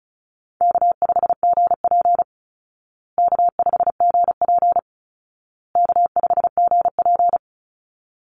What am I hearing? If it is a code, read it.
Morse code, heard as a keyed tone, about 34 wpm, 700 Hz. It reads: K5GP K5GP K5GP
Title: Call Signs Repeated 2x - 35wpm